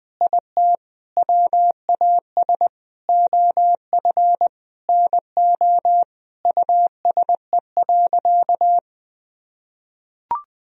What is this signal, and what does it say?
Morse code, heard as a keyed tone, about 20 wpm, 700 Hz. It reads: IT WAS OF NO USE.
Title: Sentences from Top 1000 Words - Review Part 4 - 20wpm